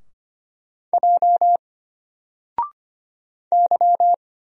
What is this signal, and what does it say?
Morse code, heard as a keyed tone, about 25 wpm, 700 Hz. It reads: J Y
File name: Single-Letter-Number-25wpm